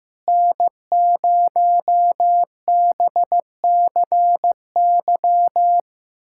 Morse code, heard as a keyed tone, about 15 wpm, 700 Hz. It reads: N0BCY
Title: USA Callsign DE Callsign 15wpm